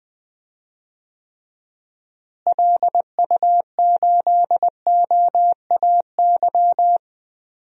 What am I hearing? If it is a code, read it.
Morse code, heard as a keyed tone, about 20 wpm, 700 Hz. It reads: LU8OAY
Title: DXCC-100 Call Signs - 6 Characters - Repeated 3x - 20wpm